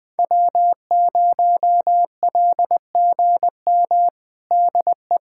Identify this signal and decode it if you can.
Morse code, heard as a keyed tone, about 21 wpm, 700 Hz. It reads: W0LGM DE
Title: USA Callsign DE Callsign - 5 Characters - 20wpm